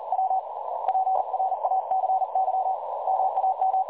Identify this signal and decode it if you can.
Morse code, heard as a keyed tone, about 41 wpm, 760 Hz, with noise. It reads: W PRACY MAM
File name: dlazaawansowanych / nagraniazpasma / websdr recording 3565.0kHz